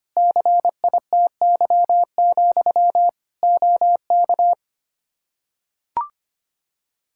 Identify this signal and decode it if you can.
Morse code, heard as a keyed tone, about 25 wpm, 700 Hz. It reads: CITY, OK
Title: Practice QSO - Exchange 1 - Beginning - 25wpm